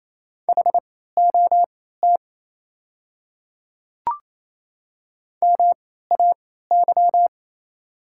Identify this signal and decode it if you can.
Morse code, heard as a keyed tone, about 28 wpm, spaced as for 16 wpm, 700 Hz. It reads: HOT MAY
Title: 3x Character Spacing - Top 100 Words - 28wpm